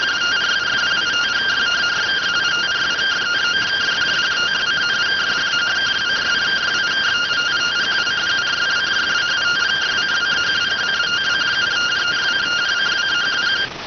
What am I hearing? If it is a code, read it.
Two-tone teleprinter signal, about 170 Hz shift.